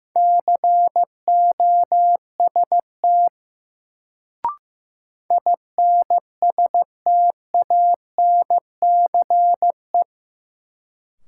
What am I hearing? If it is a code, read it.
Morse code, heard as a keyed tone, about 15 wpm, 700 Hz. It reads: COST INSTANCE
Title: Sentences from Top 800 Words - New Word Drill - 15wpm